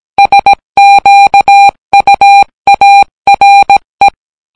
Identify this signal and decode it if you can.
Morse code, heard as a keyed tone, about 17 wpm, 800 Hz. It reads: SQUARE